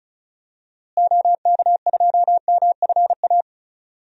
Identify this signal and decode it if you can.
Morse code, heard as a keyed tone, about 36 wpm, 700 Hz. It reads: OK2MFA
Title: DXCC-100 Call Signs - 6 Characters - Repeated 3x - 35wpm